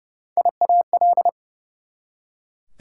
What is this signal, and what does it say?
Morse code, heard as a keyed tone, about 31 wpm, 700 Hz. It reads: IAL
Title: Prefixes and Suffixes 30wpm